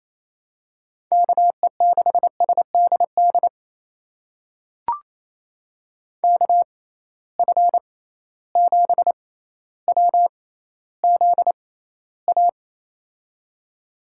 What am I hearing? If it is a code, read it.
Morse code, heard as a keyed tone, about 27 wpm, spaced as for 14 wpm, 700 Hz. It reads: KE6SDD KF7WZA
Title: USA Call Signs - 6 Characters - 6x Character Spacing - 28wpm